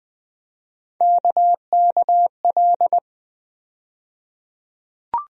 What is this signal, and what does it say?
Morse code, heard as a keyed tone, about 20 wpm, 700 Hz. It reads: KKL
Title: USA Call Sign Postfixes - 3 Letters 20wpm